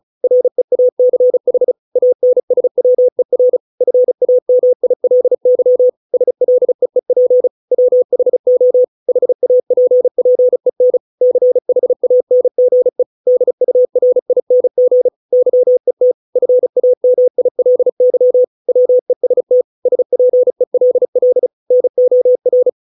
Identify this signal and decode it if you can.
Morse code, heard as a keyed tone, about 34 wpm, 500 Hz. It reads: REACH ANSWER FAMILY SLEEP WHO HAPPEN CHANGE DURING YET FAMILY WEST SPELL NOR